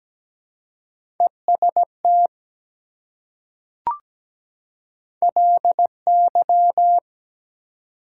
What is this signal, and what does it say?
Morse code, heard as a keyed tone, about 17 wpm, 700 Hz. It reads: EST LY
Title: Prefixes and Suffixes 17wpm